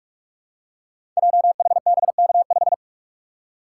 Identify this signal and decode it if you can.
Morse code, heard as a keyed tone, about 44 wpm, 690 Hz. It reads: JHBK5
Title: Code Groups - 5 Letter:Numbers - 45wpm